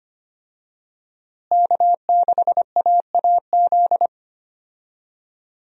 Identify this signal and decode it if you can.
Morse code, heard as a keyed tone, about 25 wpm, 700 Hz. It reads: K6AAZ
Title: USA Call Signs - 5 Characters - Repeated 3x - 25wpm